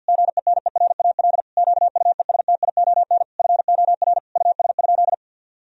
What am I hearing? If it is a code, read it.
Morse code, heard as a keyed tone, about 50 wpm, 700 Hz. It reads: GENERAL QUESTION FOR US?
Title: Sentences from Top 1000 Words - Review Part 3 - 50wpm